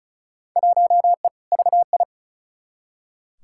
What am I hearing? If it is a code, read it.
Morse code, heard as a keyed tone, about 36 wpm, 700 Hz. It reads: 1E VI
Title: Field Day Exchanges Encore - 35wpm